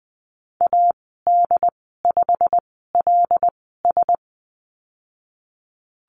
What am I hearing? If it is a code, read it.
Morse code, heard as a keyed tone, about 20 wpm, spaced as for 15 wpm, 700 Hz. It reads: AD5LS
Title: 128 Lesson 21g - Number 6 - Call Signs 20wpm